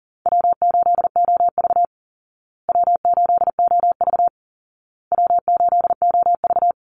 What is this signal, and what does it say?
Morse code, heard as a keyed tone, about 40 wpm, 700 Hz. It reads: W8OV W8OV W8OV
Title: Easy Call Signs Repeated 3x - 40wpm